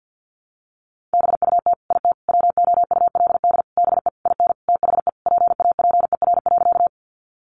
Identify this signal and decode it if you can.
Morse code, heard as a keyed tone, about 48 wpm, 700 Hz. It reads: BUT IT WOULD BE IN THE PAPER.
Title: Sentences from Top 300 Words 50wpm